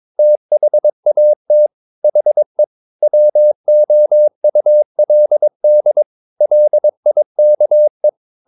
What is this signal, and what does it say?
Morse code, heard as a keyed tone, about 22 wpm, 600 Hz. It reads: THAT HE WOULD LIKE
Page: Romanen Never af Ken Follet på engelsk.